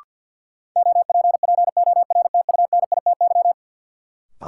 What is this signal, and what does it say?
Morse code, heard as a keyed tone, about 49 wpm, 700 Hz. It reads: OPPORTUNITY
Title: Sentences from Top 1000 Words - New Word Drill - 50wpm